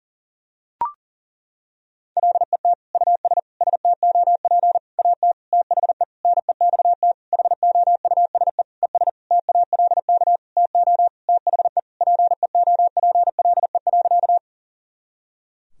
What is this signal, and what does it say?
Morse code, heard as a keyed tone, about 37 wpm, 700 Hz. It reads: LET US STOP AT THE NEXT HOUSE ES TALK TO THE PEOPLE.
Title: Sentences from Top 400 Words and Abbreviations 40wpm